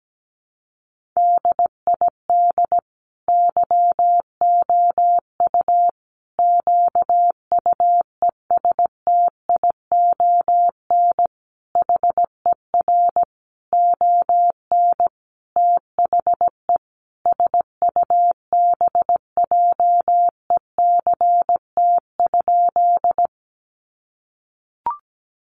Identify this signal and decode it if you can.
Morse code, heard as a keyed tone, about 17 wpm, 700 Hz. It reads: DID YOU QUESTION HER ON THE SUBJECT?
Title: Sentences from Top 800 Words 17wpm